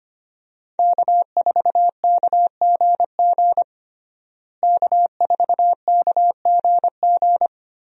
Morse code, heard as a keyed tone, about 25 wpm, 700 Hz. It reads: K4KGG K4KGG
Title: USA Call Signs - 5 Characters - Repeated 2x - 25wpm